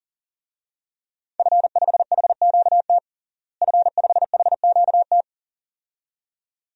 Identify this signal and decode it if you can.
Morse code, heard as a keyed tone, about 40 wpm, 700 Hz. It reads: F5HQT F5HQT
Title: DXCC-100 Call Signs - 5 Characters - Repeated 2x - 40wpm